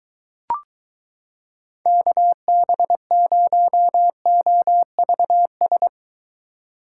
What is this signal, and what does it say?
Morse code, heard as a keyed tone, about 23 wpm, 700 Hz. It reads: KB0OVS